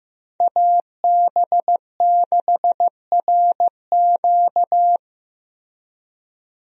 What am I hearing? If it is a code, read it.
Morse code, heard as a keyed tone, about 15 wpm, 700 Hz. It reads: AB6RQ
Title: USA Call Signs - 5 Characters - Repeated 3x - 15wpm